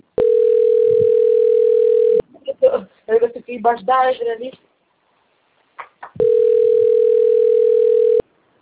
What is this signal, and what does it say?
Ringback tone (North America), 440 + 480 Hz, with speech.
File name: out- - - - - .72079